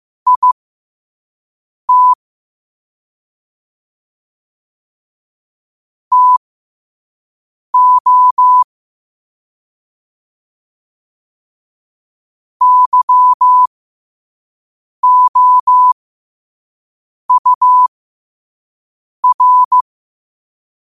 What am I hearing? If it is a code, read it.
Morse code, heard as a keyed tone, about 15 wpm, spaced as for 5 wpm, 1000 Hz. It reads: IT TO YOUR
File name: qotdfiles / 5wpm